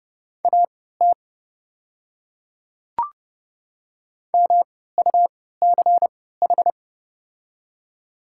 Morse code, heard as a keyed tone, about 30 wpm, spaced as for 17 wpm, 690 Hz. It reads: AT MUCH
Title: 3x Character Spacing - Top 200 Words - 30wpm